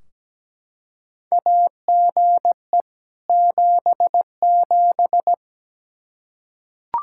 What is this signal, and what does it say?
Morse code, heard as a keyed tone, about 17 wpm, 700 Hz. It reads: AGE 77
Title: Practice QSO - Exchange 3 - More Info - 17wpm